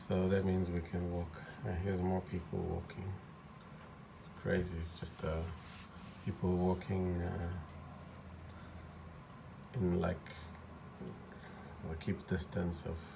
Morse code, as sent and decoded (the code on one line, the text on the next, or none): none